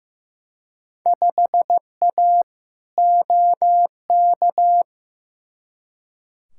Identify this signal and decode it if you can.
Morse code, heard as a keyed tone, about 15 wpm, 700 Hz. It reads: 5A OK
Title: Field Day Exchanges - 15wpm